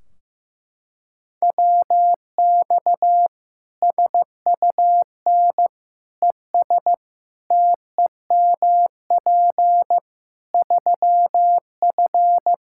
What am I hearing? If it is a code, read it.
Morse code, heard as a keyed tone, about 15 wpm, 700 Hz. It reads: WX SUN ES TEMP 3F
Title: K1USN SST QSO Practice 15wpm